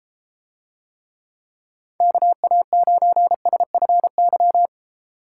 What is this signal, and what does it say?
Morse code, heard as a keyed tone, about 32 wpm, 700 Hz. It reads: KA9SFY